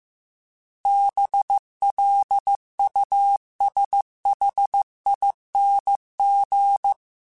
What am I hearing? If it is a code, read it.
Morse code, heard as a keyed tone, about 15 wpm, 800 Hz. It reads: BLUSHING
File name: Special INGs-mp3s / Special INGs-mp3s / Sngl ing @13